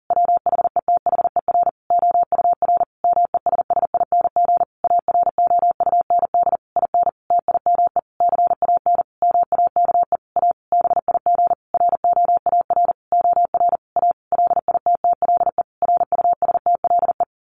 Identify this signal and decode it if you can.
Morse code, heard as a keyed tone, about 40 wpm, 700 Hz. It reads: WHETHER OUR MESSING AROUND IN TIME CAN MAKE A BIG ROAR OR A LITTLE RUSTLE
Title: A Sound Of Thunder 40wpm